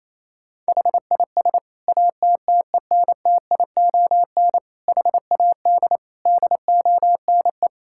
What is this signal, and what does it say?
Morse code, heard as a keyed tone, about 28 wpm, 700 Hz. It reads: HIS ATTENTION HAD DONE
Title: Sentences from Top 1000 Words - Review Part 6 - 28wpm